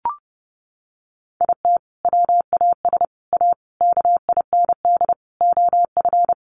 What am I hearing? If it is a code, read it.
Morse code, heard as a keyed tone, about 30 wpm, 700 Hz. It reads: IT WAS A KIND OF
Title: Sentences from Top 500 Words 30wpm